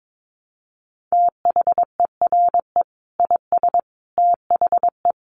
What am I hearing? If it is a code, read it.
Morse code, heard as a keyed tone, about 22 wpm, 700 Hz. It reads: THERE IS THE
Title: Sentences from Top 1500 Words 22wpm